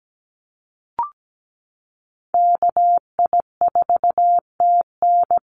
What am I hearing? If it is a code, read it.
Morse code, heard as a keyed tone, about 17 wpm, 700 Hz. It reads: KI4TN
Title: NA SOTA Activator Callsigns 17wpm